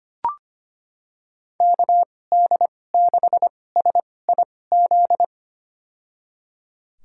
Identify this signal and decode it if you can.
Morse code, heard as a keyed tone, about 25 wpm, spaced as for 18 wpm, 700 Hz. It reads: KD6SIZ
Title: USA Call Signs - 6 Characters - 2x Character Spacing - 25wpm